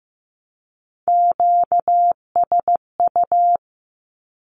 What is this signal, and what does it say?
Morse code, heard as a keyed tone, about 15 wpm, 700 Hz. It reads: QSU
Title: Q-Codes with Spoken Meaning - 15wpm